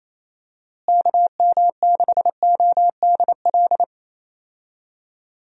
Morse code, heard as a keyed tone, about 28 wpm, 700 Hz. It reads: KM6ODL
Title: USA Call Signs - 6 Characters - Repeated 2x - 28wpm